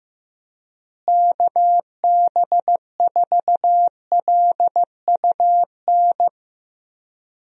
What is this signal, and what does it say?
Morse code, heard as a keyed tone, about 16 wpm, 700 Hz. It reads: KB4LUN
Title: USA Call Signs - 6 Characters - Repeated 2x - 15wpm